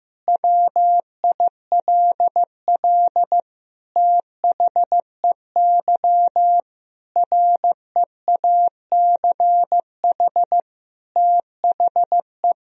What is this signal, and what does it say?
Morse code, heard as a keyed tone, about 15 wpm, 700 Hz. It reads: WILL THEY REACH THE
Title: Sentences from Top 1000 Words - Review Part 4 - 15wpm